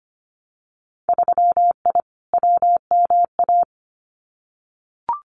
Morse code, heard as a keyed tone, about 25 wpm, 700 Hz. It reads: 3I WMA
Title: Winter Field Day Exchanges 25wpm